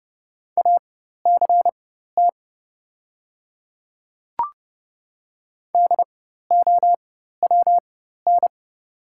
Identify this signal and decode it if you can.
Morse code, heard as a keyed tone, about 31 wpm, spaced as for 14 wpm, 700 Hz. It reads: ACT DOWN
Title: 4x Character Spacing - Top 200 Words - 30wpm